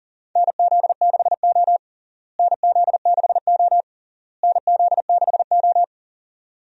Morse code, heard as a keyed tone, about 40 wpm, 700 Hz. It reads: NZ6O NZ6O NZ6O
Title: USA Call Signs - 4 Characters - Repeated 3x - 40wpm